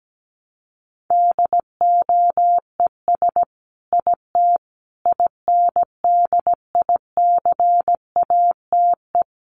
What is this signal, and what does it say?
Morse code, heard as a keyed tone, about 17 wpm, 700 Hz. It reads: DOES IT INDICATE
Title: Sentences from Top 1300 Words 17wpm